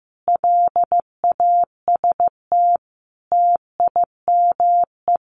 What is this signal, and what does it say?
Morse code, heard as a keyed tone, about 14 wpm, 700 Hz. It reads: LAST TIME
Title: Sets of 5 Words Encore 15wpm